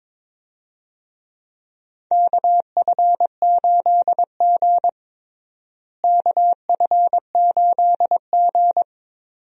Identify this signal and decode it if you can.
Morse code, heard as a keyed tone, about 22 wpm, 700 Hz. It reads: KF8G KF8G
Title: USA Call Signs - 4 Characters - Repeated 2x - 22wpm